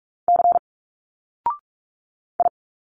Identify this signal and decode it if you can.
Morse code, heard as a keyed tone, about 44 wpm, 700 Hz. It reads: C I
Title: Single Letters 45wpm